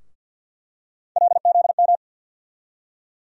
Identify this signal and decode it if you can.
Morse code, heard as a keyed tone, about 47 wpm, 700 Hz. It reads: LZM
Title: USA Call Sign Postfixes - 3 Letters 50wpm